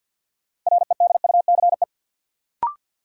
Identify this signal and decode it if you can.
Morse code, heard as a keyed tone, about 53 wpm, 700 Hz. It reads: REDUCE
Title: Sentences from Top 1100 Words - New Word Drill - 50wpm